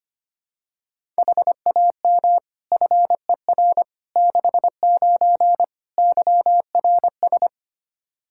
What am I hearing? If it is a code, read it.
Morse code, heard as a keyed tone, about 25 wpm, 700 Hz. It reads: HAM FER 69 YRS
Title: K1USN SST QSO Practice 25wpm